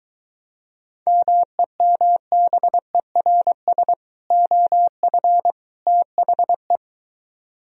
Morse code, heard as a keyed tone, about 23 wpm, 700 Hz. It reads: MEMBERS OF THE